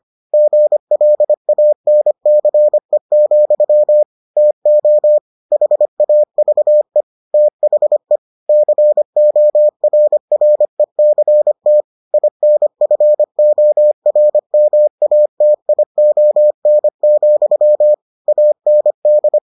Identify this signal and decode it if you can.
Morse code, heard as a keyed tone, about 25 wpm, 600 Hz. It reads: GLANCE, TO HAVE THE CORRECT INFORMATION, AND